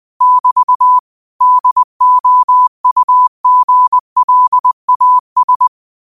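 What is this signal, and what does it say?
Morse code, heard as a keyed tone, about 20 wpm, 1000 Hz. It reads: <BT> DOUGLAS
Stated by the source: Morse quote practice at 20 wpm.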